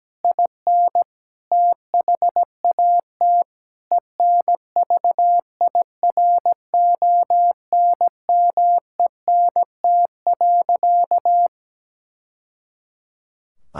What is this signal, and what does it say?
Morse code, heard as a keyed tone, about 17 wpm, 700 Hz. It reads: IN THAT ENVIRONMENT.
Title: Sentences from Top 1000 Words - Review Part 5 - 17wpm